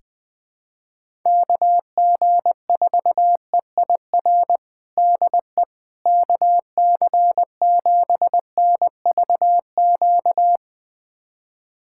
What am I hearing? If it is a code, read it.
Morse code, heard as a keyed tone, about 19 wpm, 700 Hz. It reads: KG4EIR DE KC7NVQ
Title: USA Callsign DE Callsign 20wpm